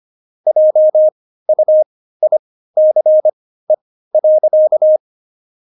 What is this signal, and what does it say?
Morse code, heard as a keyed tone, about 25 wpm, spaced as for 15 wpm, 620 Hz. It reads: JUICE.
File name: Jokes-25-char-15-eff